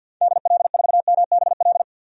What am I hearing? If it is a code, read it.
Morse code, heard as a keyed tone, about 52 wpm, 700 Hz. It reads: DL4MBL
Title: DXCC-100 Call Signs - 6 Characters - 50wpm